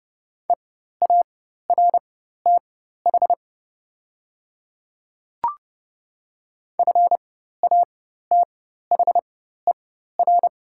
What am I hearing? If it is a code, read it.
Morse code, heard as a keyed tone, about 30 wpm, spaced as for 14 wpm, 690 Hz. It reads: EARTH FATHER
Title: 4x Character Spacing - Top 200 Words - 30wpm